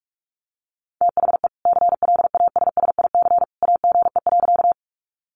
Morse code, heard as a keyed tone, about 44 wpm, 700 Hz. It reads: THE CLASSIC AGE.
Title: Sentences from Top 1000 Words 45wpm